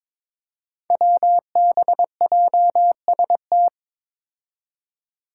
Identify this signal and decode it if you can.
Morse code, heard as a keyed tone, about 22 wpm, 700 Hz. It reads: WBJST
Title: Five Challenging Letter-Number Combinations 22wpm